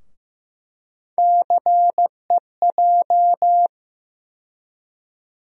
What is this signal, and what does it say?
Morse code, heard as a keyed tone, about 15 wpm, 700 Hz. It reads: CEJ
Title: USA Call Sign Postfixes - 3 Letters 15wpm